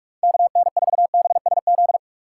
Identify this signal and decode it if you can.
Morse code, heard as a keyed tone, about 45 wpm, 700 Hz. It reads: KN4BSZ
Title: USA Call Signs - 6 Characters - 4x Character Spacing - 45wpm